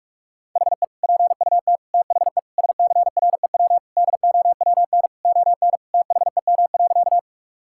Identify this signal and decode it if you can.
Morse code, heard as a keyed tone, about 46 wpm, 700 Hz. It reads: HE PUT THE SCREW DOWN ON THEM.
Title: Sentences from Top 1600 Words 45wpm